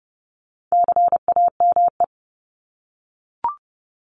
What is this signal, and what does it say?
Morse code, heard as a keyed tone, about 30 wpm, 700 Hz. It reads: CAME
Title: Top 200 Words Repeated 30wpm